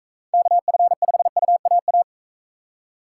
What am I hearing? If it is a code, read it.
Morse code, heard as a keyed tone, about 42 wpm, 700 Hz. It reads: KF5UAA